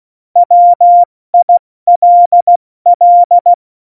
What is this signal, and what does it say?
Morse code, heard as a keyed tone, about 16 wpm, 700 Hz. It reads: WILL
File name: Morsekurs / Witze / Witz 08. Dezember 2020, 70 F80